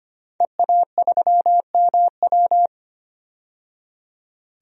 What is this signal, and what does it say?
Morse code, heard as a keyed tone, about 25 wpm, 700 Hz. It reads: EA3MW
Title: DXCC-100 Call Signs - 5 Characters - 25wpm